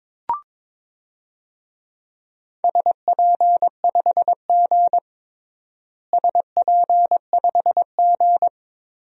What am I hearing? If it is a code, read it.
Morse code, heard as a keyed tone, about 22 wpm, 700 Hz. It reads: SP5G SP5G
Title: DXCC-100 Call Signs - 4 Characters - Repeated 3x - 22wpm